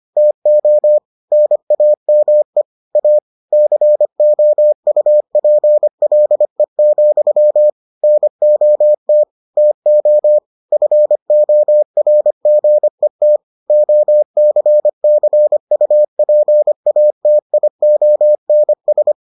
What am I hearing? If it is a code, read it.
Morse code, heard as a keyed tone, about 25 wpm, 600 Hz. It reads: TO NAME A COUPLE, NOT TO FORGET OCCUPATIONS